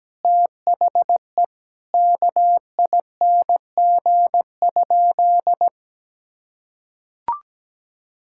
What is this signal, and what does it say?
Morse code, heard as a keyed tone, about 17 wpm, 700 Hz. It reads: THE KING?
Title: Sentences from Top 1000 Words - Review Part 6 - 17wpm